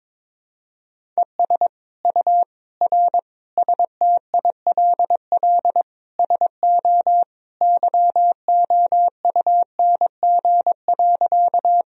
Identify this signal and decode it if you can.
Morse code, heard as a keyed tone, about 22 wpm, 700 Hz. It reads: ES U R STILL SO YOUNG.
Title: Sentences from Top 400 Words and Abbreviations 22wpm